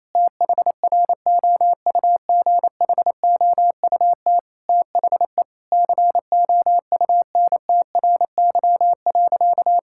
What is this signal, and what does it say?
Morse code, heard as a keyed tone, about 27 wpm, 700 Hz. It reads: THROUGHOUT THE COUNTRY.
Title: Sentences from Top 1000 Words - Review Part 5 - 28wpm